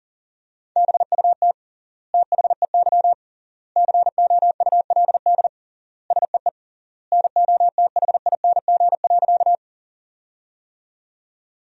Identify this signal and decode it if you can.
Morse code, heard as a keyed tone, about 40 wpm, 700 Hz. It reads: BUT THEY COULD SEE NOTHING.
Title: 3x Word-Spacing - Sentences from Top 500 Words - 40wpm